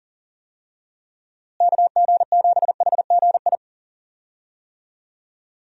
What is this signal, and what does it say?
Morse code, heard as a keyed tone, about 39 wpm, 700 Hz. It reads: KG7HGI
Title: USA Call Signs - 6 Characters - Repeated 2x - 40wpm